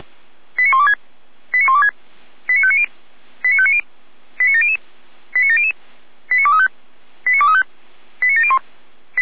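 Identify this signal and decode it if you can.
Five-tone selective call (ZVEI), heard as quick runs of five tones, tones of about 70 ms. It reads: 89126 89126 89590 89590 89700 89700 89235 89235 89701
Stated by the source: ZVEI five-tone selective call received on SDR.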